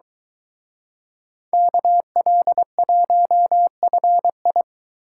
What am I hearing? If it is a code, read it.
Morse code, heard as a keyed tone, about 23 wpm, 700 Hz. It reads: KL1FI